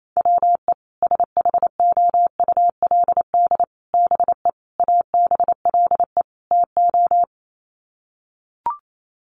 Morse code, heard as a keyed tone, about 28 wpm, 700 Hz. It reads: WE SHOULD BE ABLE TO
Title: Sets of 5 Words Encore2 28wpm